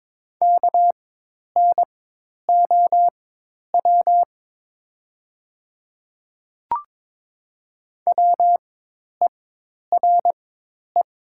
Morse code, heard as a keyed tone, about 22 wpm, spaced as for 10 wpm, 700 Hz. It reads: KNOW WERE
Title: 4x Character Spacing - Top 500 Words - 22wpm